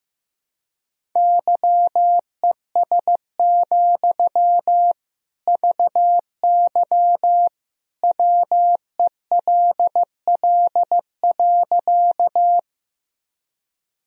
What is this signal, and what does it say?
Morse code, heard as a keyed tone, about 16 wpm, 700 Hz. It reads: YES, VY WELL.
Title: Sentences from Top 500 Words and Abbreviations 15wpm